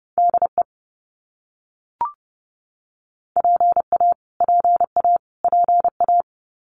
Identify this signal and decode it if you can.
Morse code, heard as a keyed tone, about 30 wpm, 700 Hz. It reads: DE PA PA PA
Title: US States Repeated 3x 30wpm